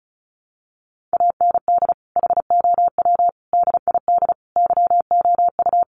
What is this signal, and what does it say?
Morse code, heard as a keyed tone, about 34 wpm, 700 Hz. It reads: AND HOW DID YOU
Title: Sentences from Top 1400 Words 35wpm